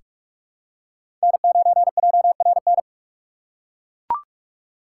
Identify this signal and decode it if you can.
Morse code, heard as a keyed tone, about 42 wpm, 700 Hz. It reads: N9JRN
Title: USA Call Signs - 5 Characters - 4x Character Spacing - 45wpm